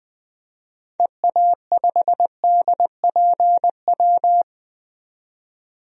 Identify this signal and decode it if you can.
Morse code, heard as a keyed tone, about 20 wpm, 700 Hz. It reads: EA5DPW
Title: DXCC-100 Call Signs - 6 Characters - 20wpm